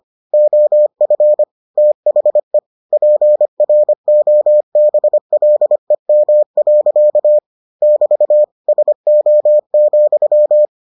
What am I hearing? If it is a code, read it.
Morse code, heard as a keyed tone, about 25 wpm, 600 Hz. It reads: OF THE PROBLEM. <BT> SO,